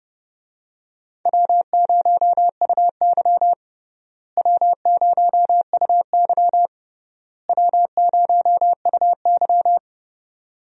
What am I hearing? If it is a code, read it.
Morse code, heard as a keyed tone, about 32 wpm, 700 Hz. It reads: W0UY W0UY W0UY
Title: USA Call Signs - 4 Characters - Repeated 3x - 30wpm